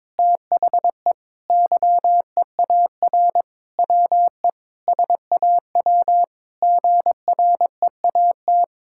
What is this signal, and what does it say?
Morse code, heard as a keyed tone, about 22 wpm, 700 Hz. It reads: THE YEAR WE SAW GREAT
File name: Sentences with 7-Word Phrases - 22wpm